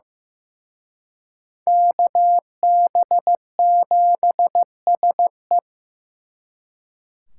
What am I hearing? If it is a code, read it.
Morse code, heard as a keyed tone, about 18 wpm, 700 Hz. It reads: KB7SE
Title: Speed Racing - USA 2x2 Call Signs 15wpm